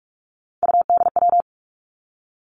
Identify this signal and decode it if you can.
Morse code, heard as a keyed tone, about 46 wpm, 700 Hz. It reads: UDW
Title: Three Challenging Letter-Number Combinations 45wpm